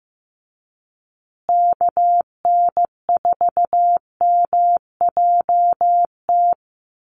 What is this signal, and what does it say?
Morse code, heard as a keyed tone, about 15 wpm, 700 Hz. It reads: KN4MJT
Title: USA Call Signs Repeated 3x 15wpm